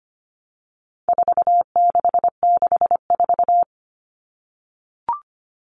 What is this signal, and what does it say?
Morse code, heard as a keyed tone, about 25 wpm, 700 Hz. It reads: 4664
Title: All Cut Numbers - 4-Digit 25wpm